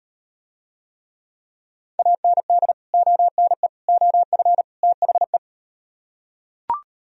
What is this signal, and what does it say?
Morse code, heard as a keyed tone, about 38 wpm, 700 Hz. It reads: AND ONE OF THE